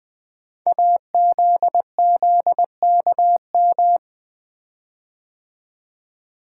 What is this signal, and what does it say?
Morse code, heard as a keyed tone, about 20 wpm, 700 Hz. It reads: AZZKM
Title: Code Groups - 5 Letters - 20wpm